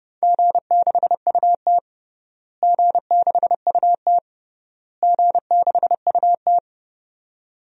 Morse code, heard as a keyed tone, about 30 wpm, 700 Hz. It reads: G6UT G6UT G6UT
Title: DXCC-100 Call Signs - 4 Characters - Repeated 3x - 30wpm